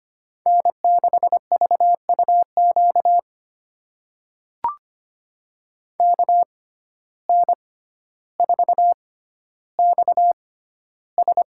Morse code, heard as a keyed tone, about 25 wpm, spaced as for 13 wpm, 700 Hz. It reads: N6VUQ KN4XS
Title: USA Call Signs - 5 Characters - 6x Character Spacing - 25wpm